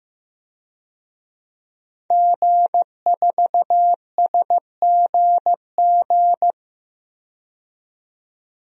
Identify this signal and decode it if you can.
Morse code, heard as a keyed tone, about 15 wpm, 700 Hz. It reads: G4SGG
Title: DXCC-100 Call Signs - 5 Characters - 15wpm